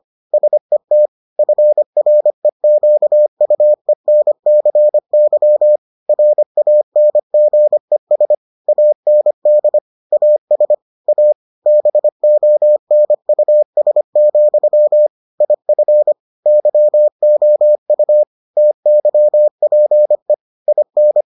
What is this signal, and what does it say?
Morse code, heard as a keyed tone, about 25 wpm, 600 Hz. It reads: SET FREQUENCY RANGES AND AS A BONUS, IF YOU TYPE IN